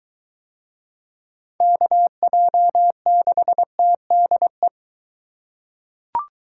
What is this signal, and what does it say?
Morse code, heard as a keyed tone, about 21 wpm, 700 Hz. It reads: KJ6TDE